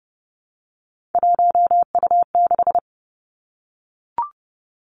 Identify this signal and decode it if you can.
Morse code, heard as a keyed tone, about 30 wpm, 700 Hz. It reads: 1U6
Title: Three Challenging Letter-Number Combinations 30wpm